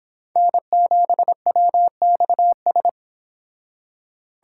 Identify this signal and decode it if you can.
Morse code, heard as a keyed tone, about 32 wpm, 700 Hz. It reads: N7WXS